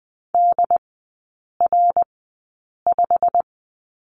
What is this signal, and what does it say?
Morse code, heard as a keyed tone, about 20 wpm, spaced as for 8 wpm, 690 Hz. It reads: DR5
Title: 087 Lesson 15e - Number 5 - Three Characters 20wpm